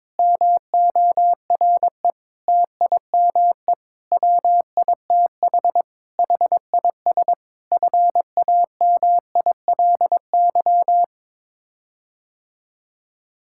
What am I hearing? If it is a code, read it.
Morse code, heard as a keyed tone, about 22 wpm, 700 Hz. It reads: MORE TIME WITH HIS FAMILY
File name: practice / 22wpm / Sets of 7 Words - 22wpm